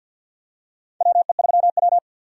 Morse code, heard as a keyed tone, about 48 wpm, 700 Hz. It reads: WE3W
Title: USA Call Signs - 4 Characters - 50wpm